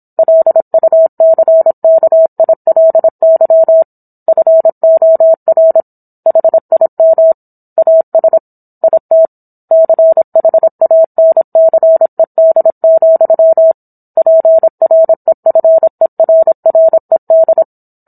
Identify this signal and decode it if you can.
Morse code, heard as a keyed tone, about 27 wpm, 650 Hz. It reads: LUCKILY FOR HIM AS IT CHANCED, PREFERRED